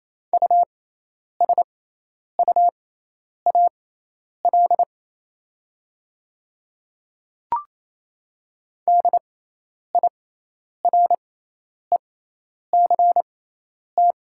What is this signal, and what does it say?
Morse code, heard as a keyed tone, about 28 wpm, spaced as for 10 wpm, 700 Hz. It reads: USUAL DIRECT
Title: 6x Character Spacing - Top 500 Words - 28wpm